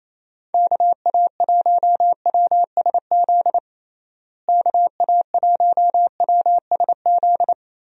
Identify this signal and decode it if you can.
Morse code, heard as a keyed tone, about 27 wpm, 700 Hz. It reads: KA1WSZ KA1WSZ
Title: USA Call Signs - 6 Characters - Repeated 2x - 28wpm